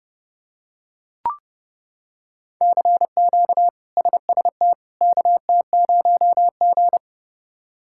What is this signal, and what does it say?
Morse code, heard as a keyed tone, about 32 wpm, 700 Hz. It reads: CQ SST KT0G
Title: K1USN SST QSO Practice - 30wpm